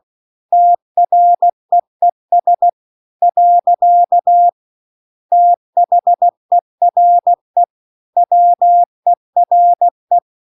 Morse code, heard as a keyed tone, about 16 wpm, 700 Hz. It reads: TREES . THERE WERE